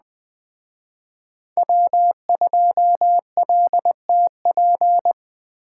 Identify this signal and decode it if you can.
Morse code, heard as a keyed tone, about 20 wpm, 700 Hz. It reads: W2LTP
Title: Speed Racing - USA Call Signs 17wpm